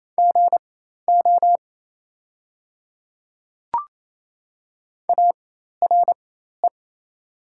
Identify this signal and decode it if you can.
Morse code, heard as a keyed tone, about 28 wpm, spaced as for 13 wpm, 700 Hz. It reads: GO ARE
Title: 4x Character Spacing - Top 100 Words - 28wpm